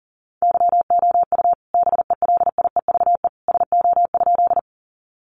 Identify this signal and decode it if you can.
Morse code, heard as a keyed tone, about 40 wpm, 700 Hz. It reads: YOU BELIEVE SO?
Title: Sentences from Top 1000 Words 40wpm